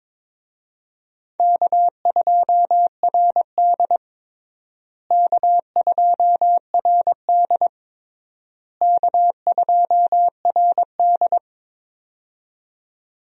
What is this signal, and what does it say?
Morse code, heard as a keyed tone, about 22 wpm, 700 Hz. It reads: K2RD K2RD K2RD
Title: USA Call Signs - 4 Characters - Repeated 3x - 22wpm